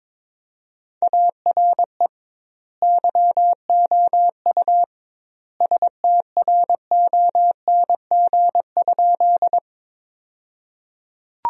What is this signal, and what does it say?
Morse code, heard as a keyed tone, about 22 wpm, 700 Hz. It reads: ARE YOU STRONG?
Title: 2x Word-Spacing - Sentences from Top 500 Words - 22wpm